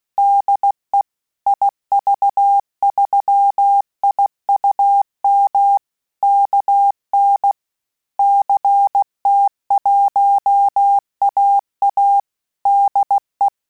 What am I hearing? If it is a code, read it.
Morse code, heard as a keyed tone, about 16 wpm, 800 Hz. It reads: DE IV3IUM KN CT1AA DE
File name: QSO CW 75 CPM